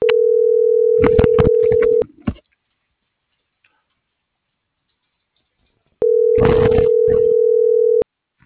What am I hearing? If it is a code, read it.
Ringback tone (North America), 440 + 480 Hz.